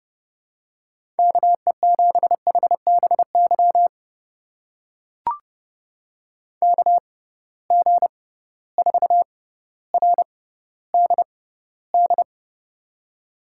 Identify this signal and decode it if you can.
Morse code, heard as a keyed tone, about 30 wpm, spaced as for 15 wpm, 700 Hz. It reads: KE7HBY KG4RDD
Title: USA Call Signs - 6 Characters - 6x Character Spacing - 30wpm